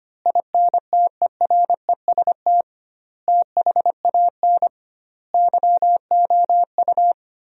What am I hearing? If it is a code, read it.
Morse code, heard as a keyed tone, about 25 wpm, 700 Hz. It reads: INTEREST THAN YOU
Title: 2x Word-Spacing - Sentences from Top 400 Words - 25wpm